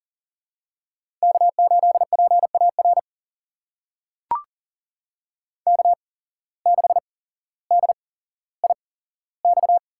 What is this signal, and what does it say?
Morse code, heard as a keyed tone, about 39 wpm, spaced as for 17 wpm, 700 Hz. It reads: K8PAR K6DIX
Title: USA Call Signs - 5 Characters - 8x Character Spacing - 40wpm